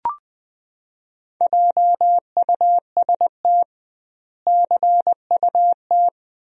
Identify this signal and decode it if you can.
Morse code, heard as a keyed tone, about 21 wpm, 700 Hz. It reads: JUST CUT
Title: 2x Word-Spacing - Sentences from Top 300 Words - 20wpm